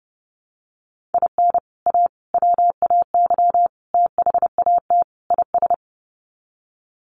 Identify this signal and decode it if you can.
Morse code, heard as a keyed tone, about 30 wpm, 700 Hz. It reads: IN A WAY THAT IS
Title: Sets of 5 Words Encore 30wpm